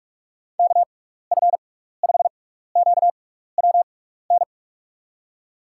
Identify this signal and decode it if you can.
Morse code, heard as a keyed tone, about 41 wpm, spaced as for 15 wpm, 700 Hz. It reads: KF5QWN
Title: USA Call Signs - 6 Characters - 6x Character Spacing - 45wpm